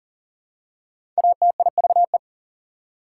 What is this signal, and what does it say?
Morse code, heard as a keyed tone, about 37 wpm, 700 Hz. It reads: ATIVE
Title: Prefixes and Suffixes 40wpm